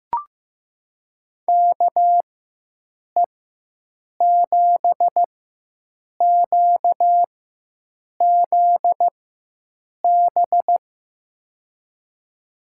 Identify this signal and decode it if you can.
Morse code, heard as a keyed tone, about 15 wpm, spaced as for 7 wpm, 700 Hz. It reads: KE7QZB
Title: USA Call Signs - 6 Characters - 4x Character Spacing - 15wpm